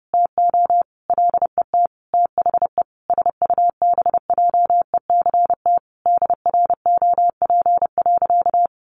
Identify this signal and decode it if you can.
Morse code, heard as a keyed tone, about 30 wpm, 700 Hz. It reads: TO LET THE SUBJECT DROP.
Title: Sentences from Top 1400 Words 30wpm